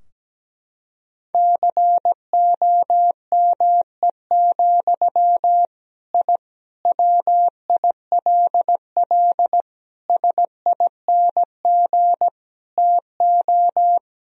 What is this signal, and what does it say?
Morse code, heard as a keyed tone, about 17 wpm, 700 Hz. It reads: COME, I WILL SING TO
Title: Sentences from Top 1000 Words - Review Part 2 - 17wpm